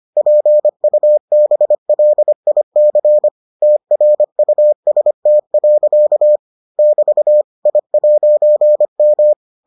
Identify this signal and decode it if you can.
Morse code, heard as a keyed tone, about 25 wpm, 600 Hz. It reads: PUBLIC TRUST. <BT> I'M